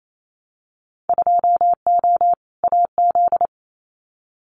Morse code, heard as a keyed tone, about 28 wpm, 700 Hz. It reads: 2O AZ
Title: Winter Field Day Exchanges 28wpm